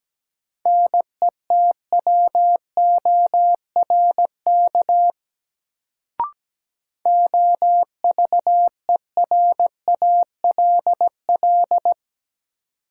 Morse code, heard as a keyed tone, about 17 wpm, 700 Hz. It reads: NETWORK OVERALL
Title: Sentences from Top 1000 Words - Review Part 7 - New Word Drill - 17wpm